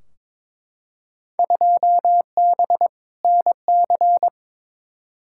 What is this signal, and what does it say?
Morse code, heard as a keyed tone, about 21 wpm, 700 Hz. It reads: 2B NC
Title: Field Day Exchanges Encore - 22wpm